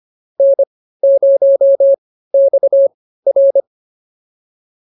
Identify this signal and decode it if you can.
Morse code, heard as a keyed tone, about 25 wpm, spaced as for 15 wpm, 590 Hz. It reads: N0XR